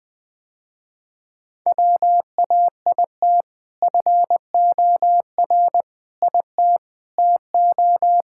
Morse code, heard as a keyed tone, about 20 wpm, 700 Hz. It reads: WAIT FOR IT TO
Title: Sentences with 2-Word Phrases - Spoken Beginning - 20wpm